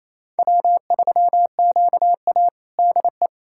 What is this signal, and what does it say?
Morse code, heard as a keyed tone, about 28 wpm, 700 Hz. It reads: W3QA DE
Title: USA Callsign DE Callsign - 4 Characters - 28wpm